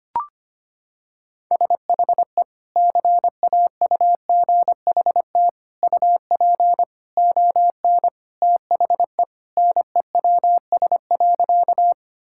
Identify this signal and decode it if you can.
Morse code, heard as a keyed tone, about 25 wpm, 700 Hz. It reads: SHE CAUGHT UP ON THE NEWS.
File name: practice / 25wpm / Sentences with 3-Word Phrases - 25wpm